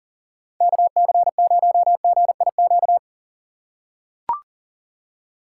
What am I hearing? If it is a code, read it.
Morse code, heard as a keyed tone, about 40 wpm, 700 Hz. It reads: KC0GIQ
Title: USA Call Signs - 6 Characters - 2x Character Spacing - 40wpm